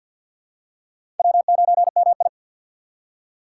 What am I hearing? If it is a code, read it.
Morse code, heard as a keyed tone, about 47 wpm, 700 Hz. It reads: W9MI
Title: USA Call Signs - 4 Characters - 6x Character Spacing - 50wpm